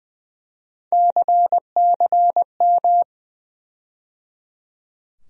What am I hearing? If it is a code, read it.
Morse code, heard as a keyed tone, about 20 wpm, 700 Hz. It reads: CCM
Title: USA Call Sign Postfixes - 3 Letters 20wpm